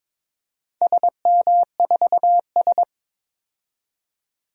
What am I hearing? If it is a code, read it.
Morse code, heard as a keyed tone, about 22 wpm, 700 Hz. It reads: SM4S
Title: DXCC-100 Call Signs - 4 Characters - 22wpm